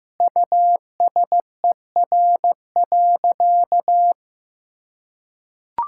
Morse code, heard as a keyed tone, about 15 wpm, 700 Hz. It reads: USER.
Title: Sentences from Top 1000 Words - Review Part 5 - 15wpm